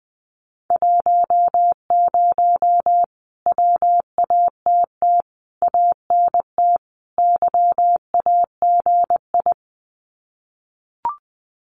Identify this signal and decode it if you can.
Morse code, heard as a keyed tone, about 20 wpm, 700 Hz. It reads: 10 WATT ANT YAGI
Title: Common QSO Phrases Encore 20wpm